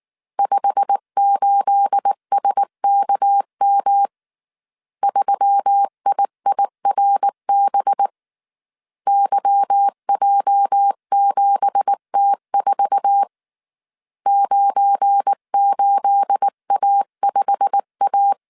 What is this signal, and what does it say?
Morse code, heard as a keyed tone, about 19 wpm, 800 Hz. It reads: 58SKM 3IIRB YJ7T4 98A5A
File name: CW / DET / 1 9 1500C S B 18wpm 2551